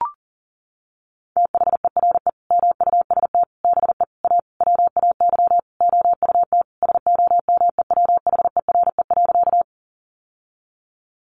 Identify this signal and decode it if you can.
Morse code, heard as a keyed tone, about 40 wpm, 700 Hz. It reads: THERE MUST BE A WAY OUT SOMEWHERE.
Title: Sentences from Top 1400 Words 40wpm